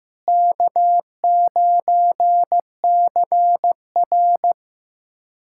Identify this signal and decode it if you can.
Morse code, heard as a keyed tone, about 15 wpm, 700 Hz. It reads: K9CR
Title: USA Callsign DE Callsign - 4 Characters - 15wpm